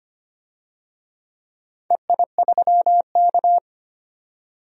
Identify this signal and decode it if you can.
Morse code, heard as a keyed tone, about 25 wpm, 700 Hz. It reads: EI3K
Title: DXCC-100 Call Signs - 4 Characters - 25wpm